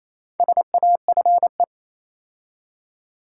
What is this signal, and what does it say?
Morse code, heard as a keyed tone, about 27 wpm, 690 Hz. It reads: SAFE
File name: Easy Words - Speed-Racing with Hints - 22wpm